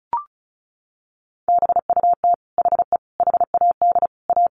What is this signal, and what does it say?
Morse code, heard as a keyed tone, about 34 wpm, 700 Hz. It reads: BUT HE HAD A
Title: Sentences from Top 1200 Words 35wpm